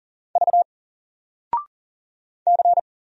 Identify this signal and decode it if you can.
Morse code, heard as a keyed tone, about 40 wpm, 700 Hz. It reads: V C
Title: Single Character (Letter:Number:Punctuation:Prosign) 40wpm